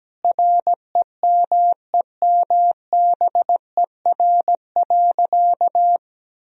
Morse code, heard as a keyed tone, about 17 wpm, 700 Hz. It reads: REMEMBER.
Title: Sentences from Top 1000 Words - Review Part 2 - 17wpm